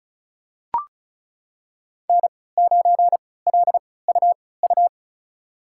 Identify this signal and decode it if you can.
Morse code, heard as a keyed tone, about 36 wpm, spaced as for 20 wpm, 690 Hz. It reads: N9LUU
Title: USA Call Signs - 5 Characters - 3x Character Spacing - 35wpm